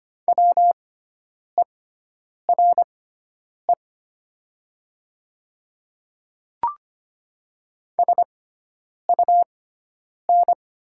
Morse code, heard as a keyed tone, about 25 wpm, spaced as for 9 wpm, 700 Hz. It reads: WERE SUN
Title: 6x Character Spacing - Top 300 Words - 25wpm